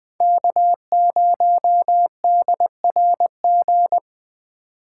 Morse code, heard as a keyed tone, about 20 wpm, 700 Hz. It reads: K0DRG